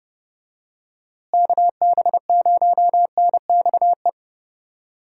Morse code, heard as a keyed tone, about 31 wpm, 700 Hz. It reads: KB0NXE
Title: USA Call Signs - 6 Characters - Repeated 3x - 30wpm